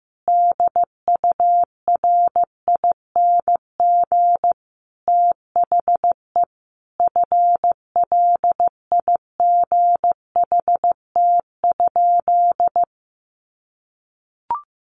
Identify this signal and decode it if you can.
Morse code, heard as a keyed tone, about 14 wpm, 700 Hz. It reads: DURING THE FLIGHT?
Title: Sentences from Top 1000 Words 15wpm